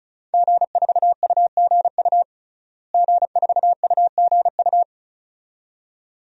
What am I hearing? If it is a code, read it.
Morse code, heard as a keyed tone, about 36 wpm, 700 Hz. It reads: G4UGU G4UGU
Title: DXCC-100 Call Signs - 5 Characters - Repeated 2x - 35wpm